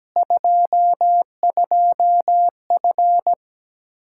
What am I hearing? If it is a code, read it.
Morse code, heard as a keyed tone, about 17 wpm, 700 Hz. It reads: 22F
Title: Practice QSO - Exchange 2 - Info - 17wpm